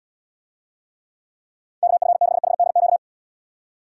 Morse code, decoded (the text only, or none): KF6HDQ